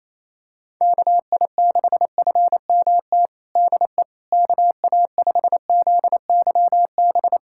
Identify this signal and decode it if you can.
Morse code, heard as a keyed tone, about 28 wpm, 700 Hz. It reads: KI6FMT DE KA5ZYB
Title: USA Callsign DE Callsign - 6 Characters - 28wpm